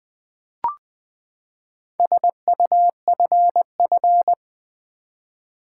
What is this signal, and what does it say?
Morse code, heard as a keyed tone, about 20 wpm, 700 Hz. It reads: SUFF
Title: Build A Word - 4 Letters At A Time - 12-Letter Words - 20wpm